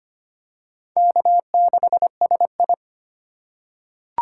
Morse code, heard as a keyed tone, about 25 wpm, 700 Hz. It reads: K6SI
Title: USA Call Signs - 2x Character Spacing - 25wpm